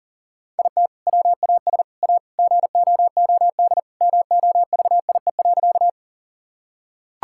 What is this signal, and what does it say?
Morse code, heard as a keyed tone, about 40 wpm, 700 Hz. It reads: IT WAS A GOOD MOVIE.
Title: Sentences from Top 1000 Words - Review Part 5 - 40wpm